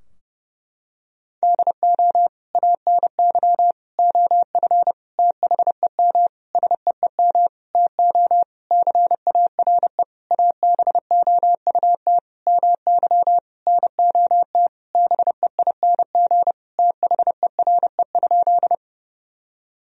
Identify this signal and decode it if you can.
Morse code, heard as a keyed tone, about 30 wpm, 700 Hz. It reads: DO ANY OF THEM SEEM TO CARE ABOUT MY NOT BEING THERE?
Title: Sentences from Top 1000 Words - Review Part 3 - 30wpm